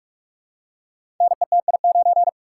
Morse code, heard as a keyed tone, about 44 wpm, 700 Hz. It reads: NETI9
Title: Code Groups - 5 Letter:Numbers - 45wpm